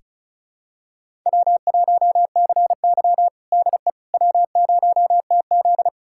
Morse code, heard as a keyed tone, about 36 wpm, 700 Hz. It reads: W1CY DE W0TZ
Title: USA Callsign DE Callsign - 4 Characters - 35wpm